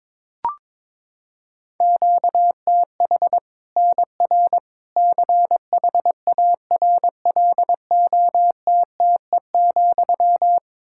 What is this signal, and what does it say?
Morse code, heard as a keyed tone, about 21 wpm, 700 Hz. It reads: QTH NR CHARLOTTE,
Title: Practice QSO - Exchange 1 - Beginning - 22wpm